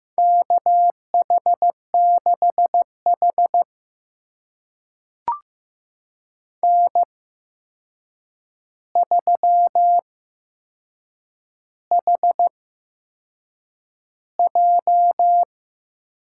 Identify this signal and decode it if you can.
Morse code, heard as a keyed tone, about 15 wpm, spaced as for 6 wpm, 700 Hz. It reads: KH6H N3HJ
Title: USA Call Signs - 4 Characters - 8x Character Spacing - 15wpm